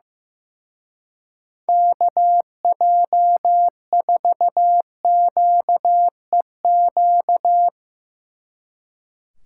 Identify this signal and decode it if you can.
Morse code, heard as a keyed tone, about 16 wpm, 700 Hz. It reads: KJ4QEQ
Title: Speed Racing - USA 2x3 Call Signs 15wpm